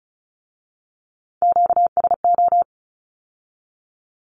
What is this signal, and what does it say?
Morse code, heard as a keyed tone, about 36 wpm, 700 Hz. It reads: QSO
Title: Q-Codes with Spoken Meaning - 35wpm